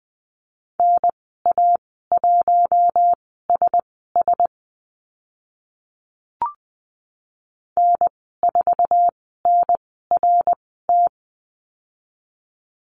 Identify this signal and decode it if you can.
Morse code, heard as a keyed tone, about 20 wpm, spaced as for 14 wpm, 700 Hz. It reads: NA1SS N4NRT
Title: 055 Lesson 10g - Letter r - Call Signs 20wpm